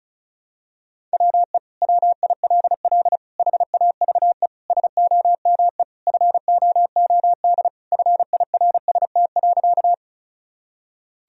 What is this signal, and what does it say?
Morse code, heard as a keyed tone, about 36 wpm, 700 Hz. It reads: WE WILL HAVE SOME FOOD FIRST.
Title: Sentences from Top 1000 Words - Review Part 1 - 35wpm